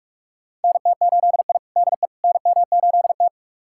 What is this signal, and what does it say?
Morse code, heard as a keyed tone, about 46 wpm, 700 Hz. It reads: NT8I DE NM8T
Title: USA Callsign DE Callsign - 4 Characters - 45wpm